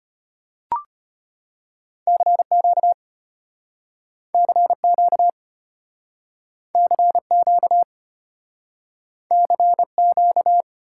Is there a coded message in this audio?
Yes: Morse code from 2.1 to 10.6 s, heard as a keyed tone, about 34 wpm, 700 Hz. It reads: CQ CQ CQ CQ